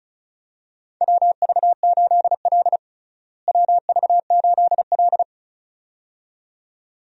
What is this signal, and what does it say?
Morse code, heard as a keyed tone, about 36 wpm, 700 Hz. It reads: WV8L WV8L
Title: USA Call Signs - 4 Characters - Repeated 2x - 35wpm